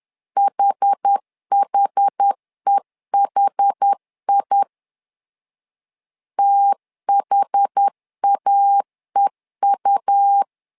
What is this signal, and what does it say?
Morse code, heard as a keyed tone, about 10 wpm, 800 Hz. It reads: HHEHI THAEU
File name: LEZIONI / 2 UHDN / 2 UH 10wpm 1714